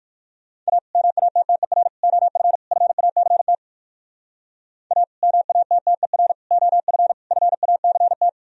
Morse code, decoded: A MATTER OF FACT A MATTER OF FACT

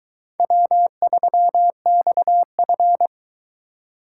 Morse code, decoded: W3XF

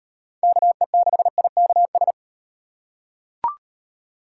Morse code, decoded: KE6IKS

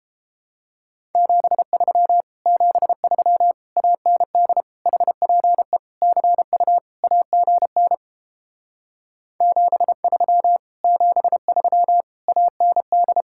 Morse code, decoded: 73 73 AND HPE CU AGN 73 73 AND

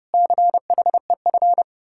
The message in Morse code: CHEF

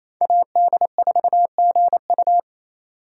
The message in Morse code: AD4GU

